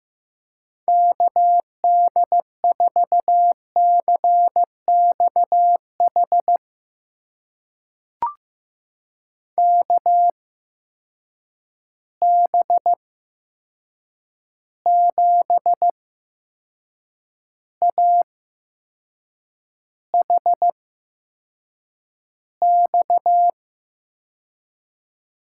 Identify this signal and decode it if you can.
Morse code, heard as a keyed tone, about 15 wpm, spaced as for 6 wpm, 700 Hz. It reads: KD4CXH KB7AHX
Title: USA Call Signs - 6 Characters - 8x Character Spacing - 15wpm